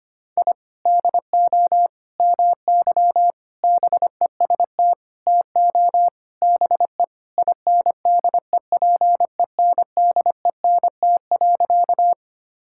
Morse code, heard as a keyed tone, about 25 wpm, 700 Hz. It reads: I DO MY BEST TO BE INDEPENDENT.
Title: Sentences from Top 1000 Words - Review Part 6 - 25wpm